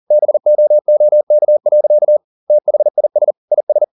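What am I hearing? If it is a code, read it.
Morse code, heard as a keyed tone, about 40 wpm, 600 Hz. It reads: BOOK. THIS IS
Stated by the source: Morse code at 40 wpm, practice text, CW med Gnister.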